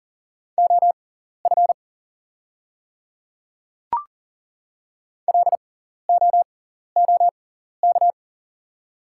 Morse code, heard as a keyed tone, about 40 wpm, spaced as for 14 wpm, 690 Hz. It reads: OF LOOK
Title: 6x Character Spacing - Top 100 Words - 40wpm